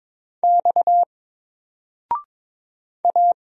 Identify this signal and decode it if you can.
Morse code, heard as a keyed tone, about 22 wpm, 700 Hz. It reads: X A
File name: Single-Letter-Number-22wpm